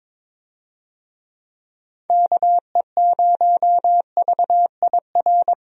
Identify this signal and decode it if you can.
Morse code, heard as a keyed tone, about 22 wpm, 700 Hz. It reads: KE0VIR
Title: USA Call Signs - 6 Characters - Repeated 3x - 22wpm